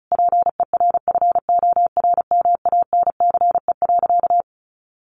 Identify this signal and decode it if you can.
Morse code, heard as a keyed tone, about 34 wpm, 700 Hz. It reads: PERFORMANCE.
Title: Sentences from Top 1000 Words 35wpm